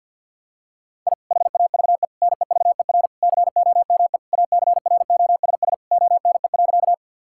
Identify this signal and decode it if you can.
Morse code, heard as a keyed tone, about 50 wpm, 700 Hz. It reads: I HAVE NEVER COME ACROSS ONE.
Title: Sentences from Top 1000 Words - Review Part 5 - 50wpm